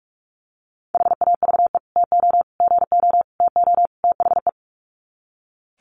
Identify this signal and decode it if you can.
Morse code, heard as a keyed tone, about 44 wpm, 700 Hz. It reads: HAVE TO GO TO THE
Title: Sets of 5 Words Encore 45wpm